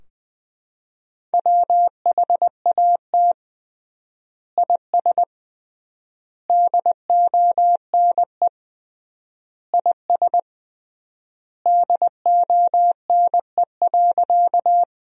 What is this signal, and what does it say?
Morse code, heard as a keyed tone, about 20 wpm, 700 Hz. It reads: WHAT IS DONE IS DONE.
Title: 3x Word-Spacing - Sentences from Top 500 Words - 20wpm